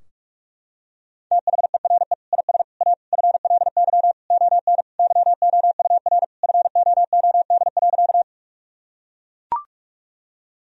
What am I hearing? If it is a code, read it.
Morse code, heard as a keyed tone, about 43 wpm, 700 Hz. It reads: THERE IS A FLY ON YOUR FOOD.
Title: Sentences from Top 1000 Words - Review Part 2 - 45wpm